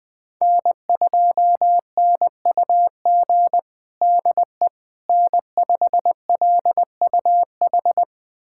Morse code, heard as a keyed tone, about 21 wpm, 700 Hz. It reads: N2NUG DE N5LUH
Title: USA Callsign DE Callsign - 5 Characters - 20wpm